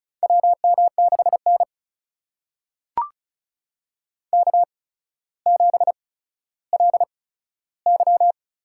Morse code, heard as a keyed tone, about 36 wpm, spaced as for 15 wpm, 700 Hz. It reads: WM6N K7LY
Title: USA Call Signs - 4 Characters - 8x Character Spacing - 35wpm